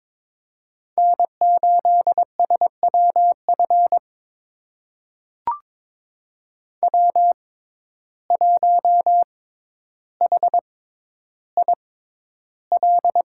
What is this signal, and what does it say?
Morse code, heard as a keyed tone, about 22 wpm, spaced as for 11 wpm, 700 Hz. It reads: N8SWF W1HIL
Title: USA Call Signs - 5 Characters - 6x Character Spacing - 22wpm